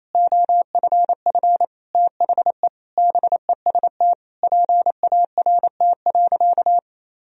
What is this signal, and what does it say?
Morse code, heard as a keyed tone, about 28 wpm, 700 Hz. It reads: OFF THE BEST PART.
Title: Sentences from Top 1000 Words - Review Part 1 - 28wpm